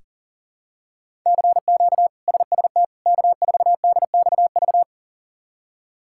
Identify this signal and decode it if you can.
Morse code, heard as a keyed tone, about 40 wpm, 700 Hz. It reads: CQ SST K4DXV
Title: K1USN SST QSO Practice - 40wpm